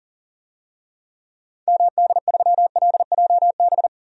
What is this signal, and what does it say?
Morse code, heard as a keyed tone, about 40 wpm, 700 Hz. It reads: MD3LJB
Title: DXCC-100 Call Signs - 6 Characters - 40wpm